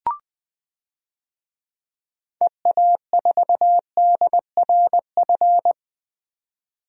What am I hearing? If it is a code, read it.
Morse code, heard as a keyed tone, about 20 wpm, 700 Hz. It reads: EA4DRF
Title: DXCC-100 Call Signs - 6 Characters - 20wpm